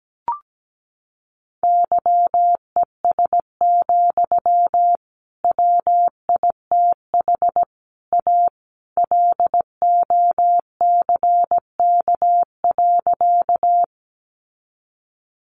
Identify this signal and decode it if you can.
Morse code, heard as a keyed tone, about 17 wpm, 700 Hz. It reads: YES, WITH A LOCK.
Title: Sentences from Top 1300 Words 17wpm